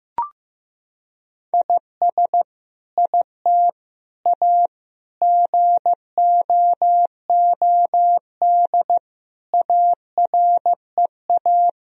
Morse code, heard as a keyed tone, about 16 wpm, 700 Hz. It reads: IS IT A GOOD AREA
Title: Sentences from Top 1000 Words - Review Part 2 - 15wpm